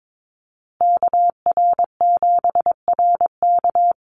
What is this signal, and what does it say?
Morse code, heard as a keyed tone, about 22 wpm, 700 Hz. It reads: KR7RK
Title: Call Signs Repeated 2x - 22wpm